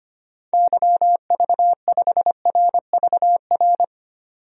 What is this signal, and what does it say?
Morse code, heard as a keyed tone, about 25 wpm, 700 Hz. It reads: YV5RVR
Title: DXCC-100 Call Signs - 6 Characters - 25wpm